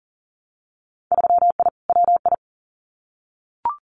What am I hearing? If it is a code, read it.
Morse code, heard as a keyed tone, about 40 wpm, 700 Hz. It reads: 3I WI
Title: Winter Field Day Exchanges 40wpm